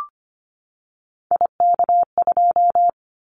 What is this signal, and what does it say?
Morse code, heard as a keyed tone, about 25 wpm, 700 Hz. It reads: IK2
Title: Callsign Prefixes for Top 50 DXCC - 25wpm